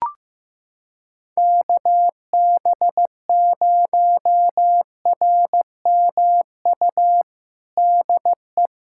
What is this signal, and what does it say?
Morse code, heard as a keyed tone, about 16 wpm, 700 Hz. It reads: KB0RMU DE
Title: USA Callsign DE Callsign - 6 Characters - 15wpm